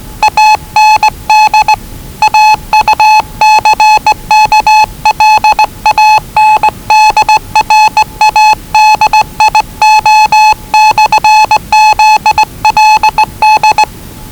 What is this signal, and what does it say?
Morse code, heard as a keyed tone, about 20 wpm, 840 Hz. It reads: AND AUCKLAND RADIO/ZLD
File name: WellingtonRadioZlwCloseDownOnHfAndMf / 01-Zlw 4277khz 417.5 qrt until 0925z